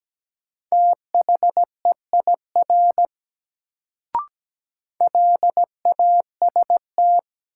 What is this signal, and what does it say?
Morse code, heard as a keyed tone, about 17 wpm, 700 Hz. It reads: THEIR LAST
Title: Sentences from Top 300 Words - New Word Drill - 17wpm